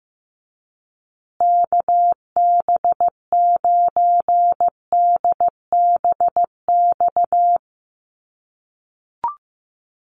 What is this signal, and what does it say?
Morse code, heard as a keyed tone, about 15 wpm, 700 Hz. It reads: KB9DBX
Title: USA Call Signs Repeated 3x 15wpm